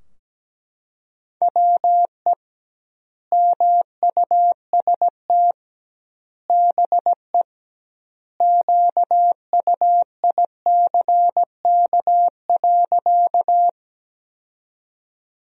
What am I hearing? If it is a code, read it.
Morse code, heard as a keyed tone, about 17 wpm, 700 Hz. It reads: WE MUST BE QUICK.
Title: 2x Word-Spacing - Sentences from Top 500 Words - 17wpm